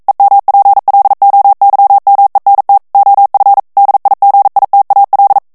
Morse code, heard as a keyed tone, about 42 wpm, 800 Hz. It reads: EMPLOYMENT OF DIGITAL